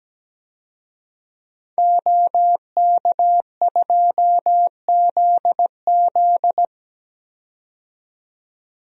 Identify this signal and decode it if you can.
Morse code, heard as a keyed tone, about 17 wpm, 700 Hz. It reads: OK2ZZ
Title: DXCC-100 Call Signs - 5 Characters - 17wpm